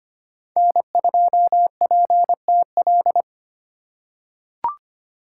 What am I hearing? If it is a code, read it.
Morse code, heard as a keyed tone, about 25 wpm, 700 Hz. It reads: N2PTL
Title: USA Call Signs - 5 Characters - 4x Character Spacing - 25wpm